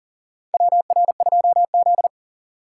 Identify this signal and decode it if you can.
Morse code, heard as a keyed tone, about 38 wpm, 700 Hz. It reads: WR2Z
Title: USA Call Signs - 4 Characters - 3x Character Spacing - 40wpm